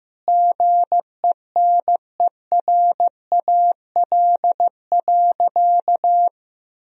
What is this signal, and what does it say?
Morse code, heard as a keyed tone, about 15 wpm, 700 Hz. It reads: GENERAL.
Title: Sentences from Top 1000 Words - Review Part 3 - 15wpm